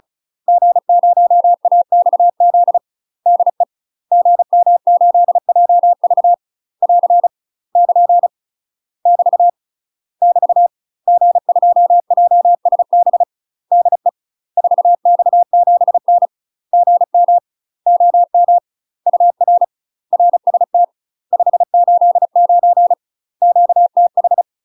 Morse code, heard as a keyed tone, about 37 wpm, 700 Hz. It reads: G0AXZ DE GM8JV <AR> <KN> <BT> <BT> G2JSB DE 4X7N GM OM UR RST 589 QTH
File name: UK MASTER QSOs TRAINING / 35WPM / UK MASTER QSOs 0021